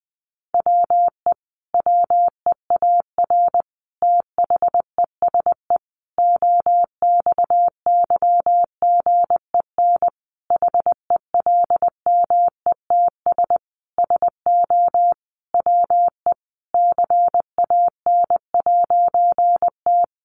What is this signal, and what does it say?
Morse code, heard as a keyed tone, about 20 wpm, 700 Hz. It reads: WE WEAR THESE OXYGEN HELMETS SO WE CAN'T
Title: A Sound Of Thunder 20wpm